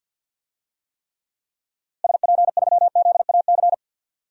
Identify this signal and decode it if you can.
Morse code, heard as a keyed tone, about 51 wpm, 700 Hz. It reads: SP3ZAC
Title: DXCC-100 Call Signs - 6 Characters - Repeated 3x - 50wpm